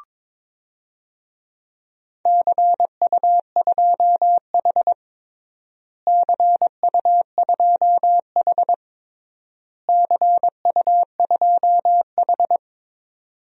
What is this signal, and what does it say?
Morse code, heard as a keyed tone, about 21 wpm, 700 Hz. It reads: CU2H CU2H CU2H
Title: DXCC-100 Call Signs - 4 Characters - Repeated 3x - 22wpm